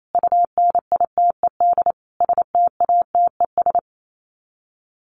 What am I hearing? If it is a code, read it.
Morse code, heard as a keyed tone, about 28 wpm, 700 Hz. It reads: UNITED STATES
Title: Sets of 5 Words Encore 28wpm